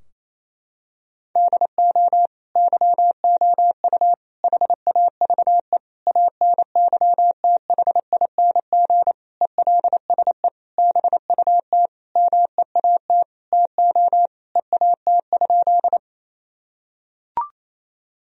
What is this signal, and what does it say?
Morse code, heard as a keyed tone, about 28 wpm, 700 Hz. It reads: DO YOU HAVE ANYTHING ELSE BUT MEAT TO EAT?
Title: Sentences from Top 1000 Words - Review Part 3 - 28wpm